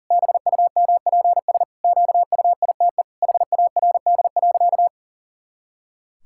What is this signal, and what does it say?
Morse code, heard as a keyed tone, about 40 wpm, 700 Hz. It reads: BUMPS QUITE HARD.
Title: Pangram Challenge v2 - 40wpm